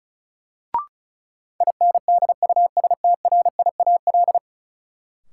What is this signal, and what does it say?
Morse code, heard as a keyed tone, about 36 wpm, 700 Hz. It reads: INDUSTRIAL
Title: Sets of 5 Words - New Word Drill - 35wpm